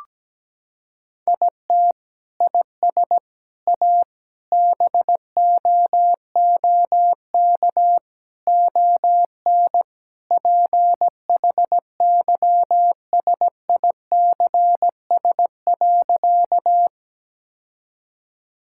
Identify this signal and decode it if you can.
Morse code, heard as a keyed tone, about 17 wpm, 700 Hz. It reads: IT IS A BOOK ON PHYSICS.
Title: Sentences from Top 1000 Words - Review Part 5 - 17wpm